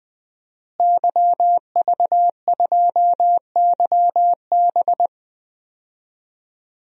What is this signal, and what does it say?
Morse code, heard as a keyed tone, about 21 wpm, 700 Hz. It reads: YV2YB
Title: DXCC-100 Call Signs - 5 Characters - Repeated 3x - 20wpm